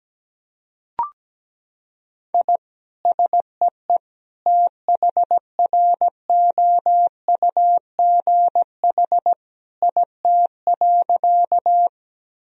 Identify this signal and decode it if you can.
Morse code, heard as a keyed tone, about 17 wpm, 700 Hz. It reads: I SEE THROUGH IT.
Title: Sentences from Top 1000 Words - Review Part 1 - 17wpm